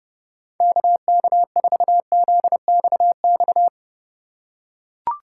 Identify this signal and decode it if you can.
Morse code, heard as a keyed tone, about 30 wpm, 700 Hz. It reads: KK4ZXX
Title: USA Call Signs - 4x Character Spacing - 30wpm